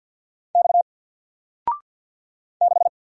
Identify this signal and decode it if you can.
Morse code, heard as a keyed tone, about 48 wpm, 700 Hz. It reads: X 6
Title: Single Character (Letter:Number:Punctuation:Prosign) 50wpm